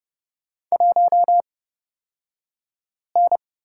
Morse code, heard as a keyed tone, about 30 wpm, spaced as for 5 wpm, 700 Hz. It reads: 1N
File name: course / 30:10wpm / 235 Lesson 38d - Letter q - Two Characters 30wpm